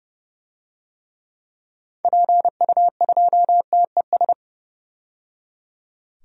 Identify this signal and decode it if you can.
Morse code, heard as a keyed tone, about 30 wpm, 700 Hz. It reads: PU2TES
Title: DXCC-100 Call Signs - 6 Characters - 30wpm